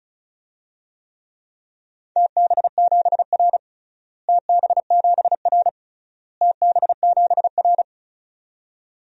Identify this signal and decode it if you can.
Morse code, heard as a keyed tone, about 34 wpm, 700 Hz. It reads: TB7R TB7R TB7R
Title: DXCC-100 Call Signs - 4 Characters - Repeated 3x - 35wpm